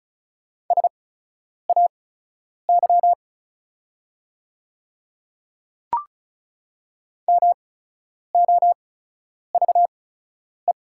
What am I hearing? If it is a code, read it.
Morse code, heard as a keyed tone, about 36 wpm, spaced as for 10 wpm, 700 Hz. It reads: SAY MOVE
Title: 8x Character Spacing - Top 200 Words - 35wpm